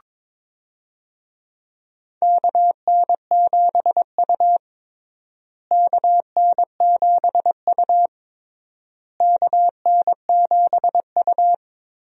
Morse code, heard as a keyed tone, about 22 wpm, 700 Hz. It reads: KN7U KN7U KN7U
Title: USA Call Signs - 4 Characters - Repeated 3x - 22wpm